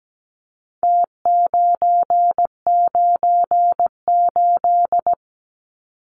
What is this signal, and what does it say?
Morse code, heard as a keyed tone, about 17 wpm, 700 Hz. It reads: T998
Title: Numbers - Contest Style - 4-Digit 17wpm